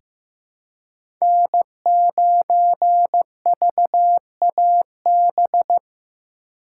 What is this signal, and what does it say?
Morse code, heard as a keyed tone, about 15 wpm, 700 Hz. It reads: N9VAB
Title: USA Call Signs - 4x Character Spacing - 15wpm